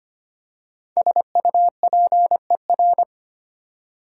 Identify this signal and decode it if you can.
Morse code, heard as a keyed tone, about 25 wpm, 700 Hz. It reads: SUPER
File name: 5 Letter Words - First 3 Letters Given - 25wpm